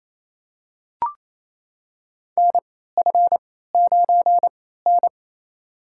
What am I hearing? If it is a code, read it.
Morse code, heard as a keyed tone, about 27 wpm, spaced as for 16 wpm, 700 Hz. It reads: NF9N
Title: USA Call Signs - 4 Characters - 3x Character Spacing - 28wpm